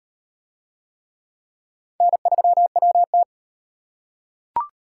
Morse code, heard as a keyed tone, about 38 wpm, 700 Hz. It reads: N3WT